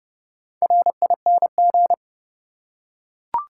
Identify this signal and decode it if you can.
Morse code, heard as a keyed tone, about 30 wpm, 700 Hz. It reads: RING
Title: Four Letter Words - 30wpm